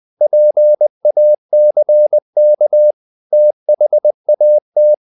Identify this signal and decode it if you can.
Morse code, heard as a keyed tone, about 20 wpm, 600 Hz. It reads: PACK THAT
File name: the old man and the sea part 2 20wpm